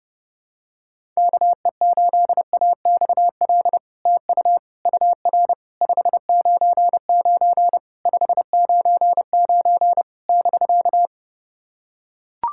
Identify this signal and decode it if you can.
Morse code, heard as a keyed tone, about 30 wpm, 700 Hz. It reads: KE8AXL TU UR 599 599 <BK>
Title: SOTA QSO Practice 30wpm